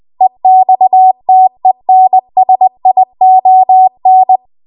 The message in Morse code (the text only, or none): EXTENSION